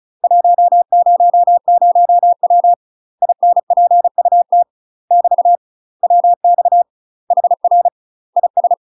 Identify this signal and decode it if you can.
Morse code, heard as a keyed tone, about 34 wpm, 700 Hz. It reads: 100W INPUT <BT> WX HR IS
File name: UK MASTER QSOs TRAINING / 35WPM / UK MASTER QSOs 0013